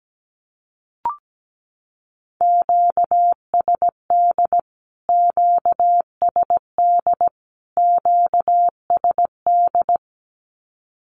Morse code, heard as a keyed tone, about 17 wpm, 700 Hz. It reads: QSD QSD QSD
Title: Q-Codes Repeated 3x 17wpm